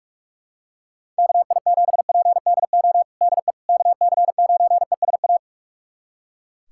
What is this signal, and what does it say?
Morse code, heard as a keyed tone, about 43 wpm, 700 Hz. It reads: KI7PDO DE KC9ESA
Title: USA Callsign DE Callsign - 6 Characters - 45wpm